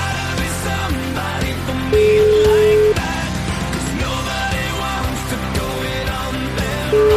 Ringback tone (Europe, 425 Hz), with noise.